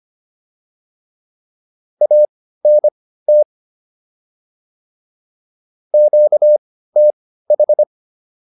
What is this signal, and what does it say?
Morse code, heard as a keyed tone, about 25 wpm, spaced as for 15 wpm, 600 Hz. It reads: ANT QTH